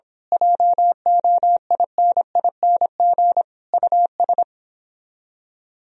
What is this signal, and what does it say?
Morse code, heard as a keyed tone, about 25 wpm, 700 Hz. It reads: JOINING US
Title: Speed-Racing - Sets of 5 Words 22wpm